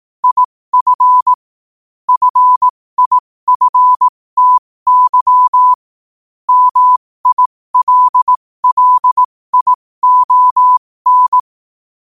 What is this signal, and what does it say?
Morse code, heard as a keyed tone, about 18 wpm, spaced as for 15 wpm, 1000 Hz. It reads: IF FIFTY MILLION
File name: qotdfiles / 15wpm